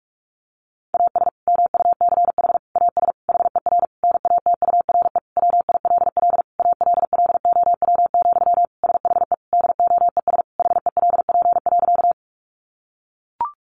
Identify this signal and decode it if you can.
Morse code, heard as a keyed tone, about 44 wpm, 700 Hz. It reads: AS MUCH AS HER NATURE WILL ALLOW, SHE DOES HELP.
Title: Sentences from Top 700 Words 45wpm